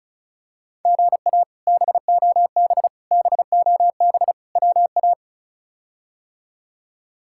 Morse code, heard as a keyed tone, about 36 wpm, 700 Hz. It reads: GA BOB BOB WA
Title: K1USN SST QSO Practice - 35wpm